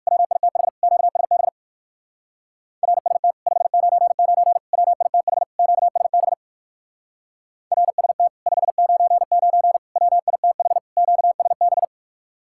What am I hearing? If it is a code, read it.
Morse code, heard as a keyed tone, about 62 wpm, 700 Hz. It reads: WITH QSB RST 599 WITH QSB RST 599 WITH QSB